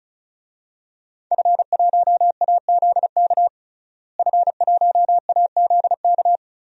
Morse code, heard as a keyed tone, about 34 wpm, 700 Hz. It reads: F1AZK F1AZK
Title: DXCC-100 Call Signs - 5 Characters - Repeated 2x - 35wpm